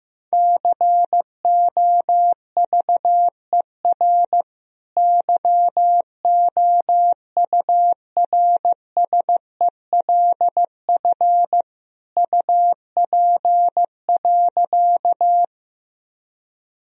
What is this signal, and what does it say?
Morse code, heard as a keyed tone, about 15 wpm, 700 Hz. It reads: COVER YOURSELF UP.
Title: Sentences from Top 1000 Words - Review Part 1 - 15wpm